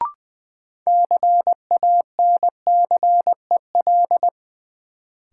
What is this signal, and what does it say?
Morse code, heard as a keyed tone, about 20 wpm, 700 Hz. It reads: CANCEL
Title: Sentences from Top 1200 Words - New Word Drill - 20wpm